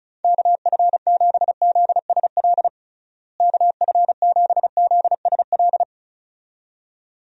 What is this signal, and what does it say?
Morse code, heard as a keyed tone, about 36 wpm, 700 Hz. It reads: KF7ZSL KF7ZSL
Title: USA Call Signs - 6 Characters - Repeated 2x - 35wpm